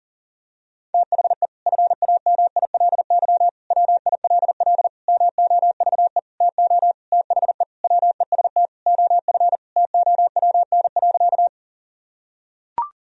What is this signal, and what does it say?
Morse code, heard as a keyed tone, about 40 wpm, 700 Hz. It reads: THE FAMILY WILL MOVE TO THE WEST OF TOWN.
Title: Sentences from Top 1000 Words - Review Part 2 - 40wpm